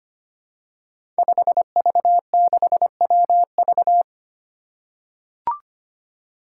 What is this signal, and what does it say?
Morse code, heard as a keyed tone, about 25 wpm, 700 Hz. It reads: 5V6WV
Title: Five Challenging Letter-Number Combinations 25wpm